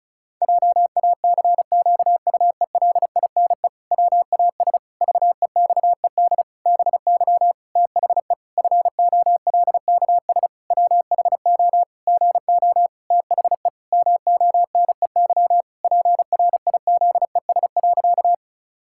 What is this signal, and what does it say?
Morse code, heard as a keyed tone, about 34 wpm, 700 Hz. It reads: JACQUELINE WAS VEXED BY THE FOLKS WHO GO THE MONEY PRIZES.
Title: Pangram Challenge v2 - 35wpm